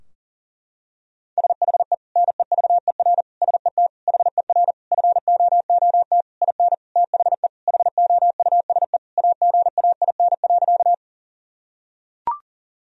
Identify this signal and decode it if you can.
Morse code, heard as a keyed tone, about 40 wpm, 700 Hz. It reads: SHE NEVER SET HER FOOT IN THE HOUSE AGAIN.
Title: Sentences from Top 1000 Words - Review Part 3 - 40wpm